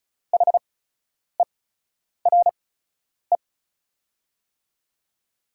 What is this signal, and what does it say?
Morse code, heard as a keyed tone, about 36 wpm, spaced as for 10 wpm, 690 Hz. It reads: HERE
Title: 8x Character Spacing - Top 200 Words - 35wpm